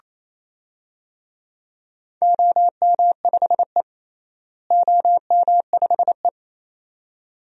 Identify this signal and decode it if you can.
Morse code, heard as a keyed tone, about 27 wpm, 700 Hz. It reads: OM5E OM5E
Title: DXCC-100 Call Signs - 4 Characters - Repeated 2x - 28wpm